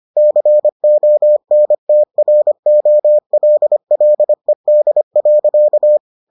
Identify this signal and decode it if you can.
Morse code, heard as a keyed tone, about 25 wpm, 600 Hz. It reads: CONTROLLED.